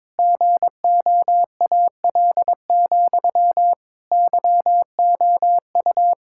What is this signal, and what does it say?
Morse code, heard as a keyed tone, about 21 wpm, 700 Hz. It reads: GOAL, YOU
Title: Sentences from Top 1000 Words - Review Part 6 - 22wpm